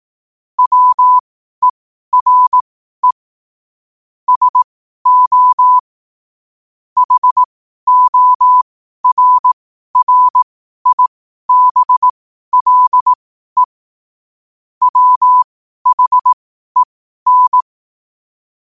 Morse code, heard as a keyed tone, about 18 wpm, spaced as for 12 wpm, 1000 Hz. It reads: WERE SO HORRIBLE WHEN
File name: qotdfiles / 12wpm